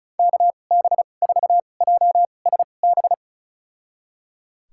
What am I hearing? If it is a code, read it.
Morse code, heard as a keyed tone, about 36 wpm, spaced as for 25 wpm, 700 Hz. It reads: KB4JSB
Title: USA Call Signs - 6 Characters - 2x Character Spacing - 35wpm